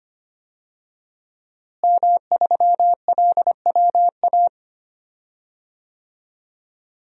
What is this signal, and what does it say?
Morse code, heard as a keyed tone, about 25 wpm, 700 Hz. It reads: M3LWA
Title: DXCC-100 Call Signs - 5 Characters - Repeated 3x - 25wpm